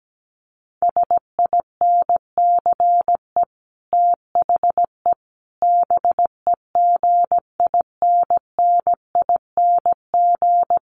Morse code, heard as a keyed tone, about 17 wpm, 700 Hz. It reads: SINCE THE BEGINNING
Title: Sets of 5 Words Encore2 17wpm